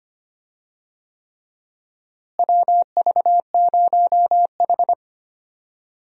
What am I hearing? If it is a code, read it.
Morse code, heard as a keyed tone, about 25 wpm, 700 Hz. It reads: WV0H
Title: USA Call Signs - 4 Characters - Repeated 3x - 25wpm